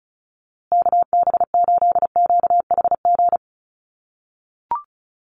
Morse code, heard as a keyed tone, about 34 wpm, 700 Hz. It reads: KB8QHG
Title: USA Call Signs Repeated 2x 35wpm